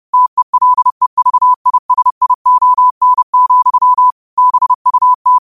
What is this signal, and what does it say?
Morse code, heard as a keyed tone, about 30 wpm, 1000 Hz. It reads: TELEVISION, BUT
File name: qotdfiles / 30wpm